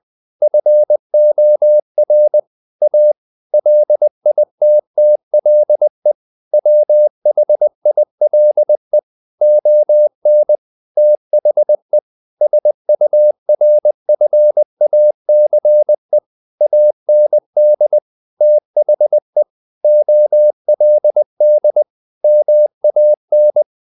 Morse code, heard as a keyed tone, about 20 wpm, 600 Hz. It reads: FOR A LITTLE WHILE ON THE SURFACE AND THE OLD MAN
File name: the old man and the sea part 2 20wpm